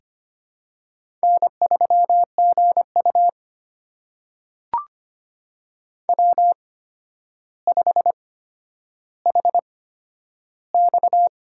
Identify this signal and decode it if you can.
Morse code, heard as a keyed tone, about 25 wpm, spaced as for 11 wpm, 700 Hz. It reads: N3GU W5HX
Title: USA Call Signs - 4 Characters - 8x Character Spacing - 25wpm